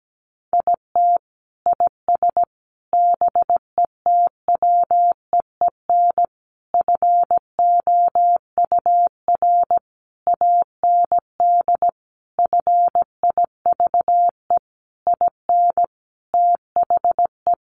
Morse code, heard as a keyed tone, about 17 wpm, 700 Hz. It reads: IT IS BETWEEN FOUR AND FIVE IN THE
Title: Sentences from Top 1500 Words 17wpm